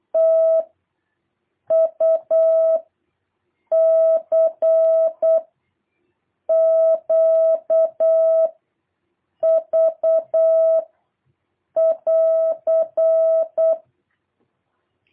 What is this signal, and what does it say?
Morse code, heard as a keyed tone, about 8 wpm, 660 Hz. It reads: T U C Q V <AR>
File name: jh8cbh / 2023cw